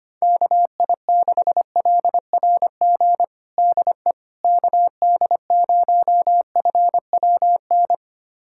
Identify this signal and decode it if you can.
Morse code, heard as a keyed tone, about 25 wpm, 700 Hz. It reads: KI6LRG DE KD0FWN
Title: USA Callsign DE Callsign - 6 Characters - 25wpm